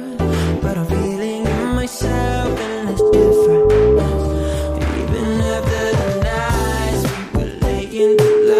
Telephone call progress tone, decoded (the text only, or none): ringback tone